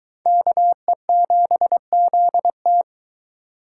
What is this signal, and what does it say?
Morse code, heard as a keyed tone, about 22 wpm, 700 Hz. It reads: KE7ZT